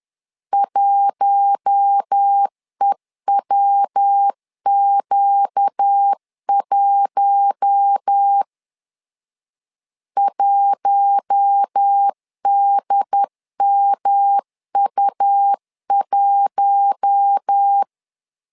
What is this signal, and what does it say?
Morse code, heard as a keyed tone, about 10 wpm, 800 Hz. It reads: 1EWQ1 1DMU1
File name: LEZIONI / 4 Q1 10wpm 2448